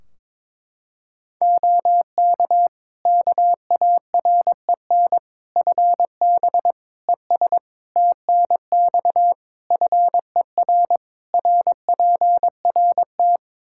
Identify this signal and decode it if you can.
Morse code, heard as a keyed tone, about 22 wpm, 700 Hz. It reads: OK KAREN FB ES TNX FER RPRT
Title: Practice QSO - Exchange 2 - Info - 22wpm